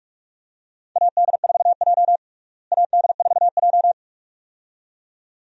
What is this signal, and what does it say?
Morse code, heard as a keyed tone, about 43 wpm, 700 Hz. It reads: AD4J AD4J
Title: USA Call Signs - 4 Characters - Repeated 2x - 45wpm